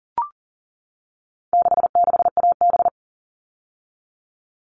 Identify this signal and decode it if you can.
Morse code, heard as a keyed tone, about 40 wpm, 700 Hz. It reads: 66AB
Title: All Cut Numbers - 4-Digit 40wpm